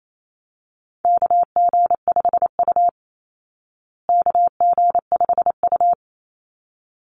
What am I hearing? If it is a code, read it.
Morse code, heard as a keyed tone, about 28 wpm, 700 Hz. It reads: KG5U KG5U
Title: Call Signs Repeated 3x - 28wpm